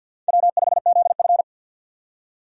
Morse code, heard as a keyed tone, about 57 wpm, 700 Hz. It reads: W5ZF
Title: Speed Racing - USA 1x2 Call Signs 50wpm